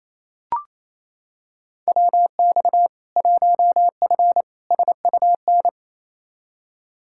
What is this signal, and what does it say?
Morse code, heard as a keyed tone, about 27 wpm, 700 Hz. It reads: WX 1F SUN
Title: Practice QSO - Exchange 2 - Info - 28wpm